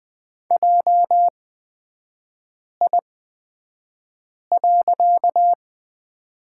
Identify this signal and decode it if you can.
Morse code, heard as a keyed tone, about 20 wpm, spaced as for 5 wpm, 700 Hz. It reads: JI.
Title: 255 Lesson 41e - Period - Three Characters 20wpm